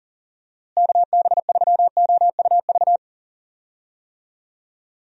Morse code, heard as a keyed tone, about 40 wpm, 700 Hz. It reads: KB3OUV
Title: USA Call Signs - 6 Characters - Repeated 2x - 40wpm